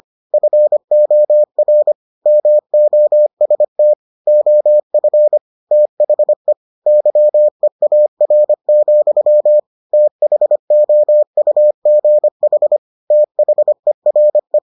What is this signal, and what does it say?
Morse code, heard as a keyed tone, about 25 wpm, 600 Hz. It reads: FOR MOST OF THE YEAR, THOUGH THERE